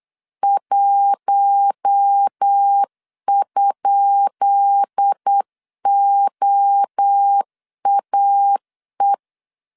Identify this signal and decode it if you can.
Morse code, heard as a keyed tone, about 8 wpm, 800 Hz. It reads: 1?OAE
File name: LEZIONI / 10 SIGNS / 10B INT 8wpm 3534